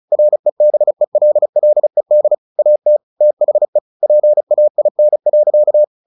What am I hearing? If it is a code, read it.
Morse code, heard as a keyed tone, about 34 wpm, 600 Hz. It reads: REBELLED AT THE PAIN.
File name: the old man and the sea 2 35wpm